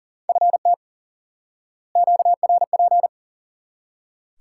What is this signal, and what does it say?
Morse code, heard as a keyed tone, about 40 wpm, 700 Hz. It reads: FT QRP
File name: IQE Course 3 - Lesson 89 - YAGI - Two Elements - 40wpm